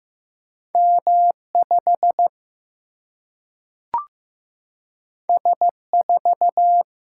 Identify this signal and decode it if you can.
Morse code, heard as a keyed tone, about 15 wpm, 700 Hz. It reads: M5 S4
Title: Single-Letter Single-Number 15wpm